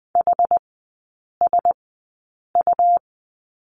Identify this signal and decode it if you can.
Morse code, heard as a keyed tone, about 20 wpm, spaced as for 8 wpm, 700 Hz. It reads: HSU
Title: 100 Lesson 17e - Letter u - Three Characters 20wpm